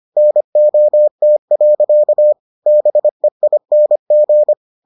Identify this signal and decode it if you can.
Morse code, heard as a keyed tone, about 25 wpm, 600 Hz. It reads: NOT. BEING